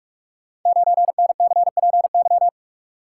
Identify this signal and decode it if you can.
Morse code, heard as a keyed tone, about 44 wpm, 690 Hz. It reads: 9NCPY
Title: Code Groups - 5 Letter:Numbers - 45wpm